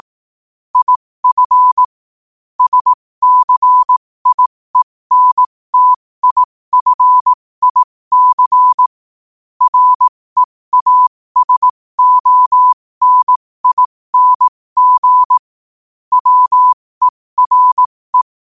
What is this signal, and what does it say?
Morse code, heard as a keyed tone, about 18 wpm, spaced as for 15 wpm, 1000 Hz. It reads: IF SCIENTIFIC REASONING WERE
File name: qotdfiles / 15wpm